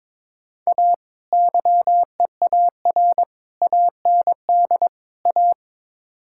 Morse code, heard as a keyed tone, about 22 wpm, 700 Hz. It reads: A YEAR AND A
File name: practice / 22wpm / Build A Phrase - 5-Word Phrases - 22wpm